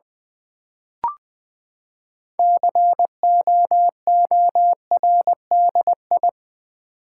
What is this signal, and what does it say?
Morse code, heard as a keyed tone, about 20 wpm, 700 Hz. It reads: COORDI
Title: Build A Word - 6 Letters At A Time - 12-Letter Words - 20wpm